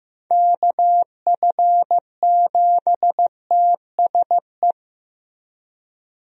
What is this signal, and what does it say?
Morse code, heard as a keyed tone, about 15 wpm, 700 Hz. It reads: KF7TSE
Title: USA Call Signs - 6 Characters - Repeated 3x - 15wpm